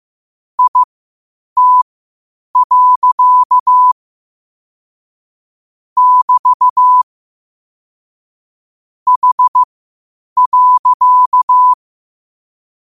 Morse code, heard as a keyed tone, about 15 wpm, spaced as for 8 wpm, 1000 Hz. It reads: IT. <BT> H.
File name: qotdfiles / 8wpm